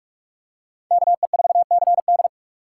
Morse code, heard as a keyed tone, about 42 wpm, 700 Hz. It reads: KE4CD
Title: USA Call Signs - 5 Characters - 6x Character Spacing - 45wpm